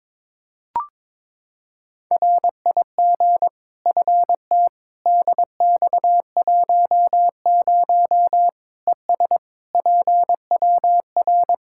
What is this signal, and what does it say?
Morse code, heard as a keyed tone, about 22 wpm, 700 Hz. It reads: RIG FT DX10 ES PWR
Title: Practice QSO - Exchange 2 - Info - 22wpm